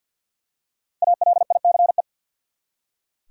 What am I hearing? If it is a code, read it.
Morse code, heard as a keyed tone, about 45 wpm, 700 Hz. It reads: ALICE
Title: USA Common Names - 50wpm